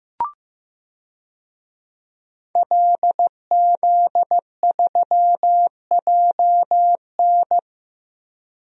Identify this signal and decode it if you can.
Morse code, heard as a keyed tone, about 15 wpm, 700 Hz. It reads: LZ3JN
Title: DXCC-100 Call Signs - 5 Characters - 15wpm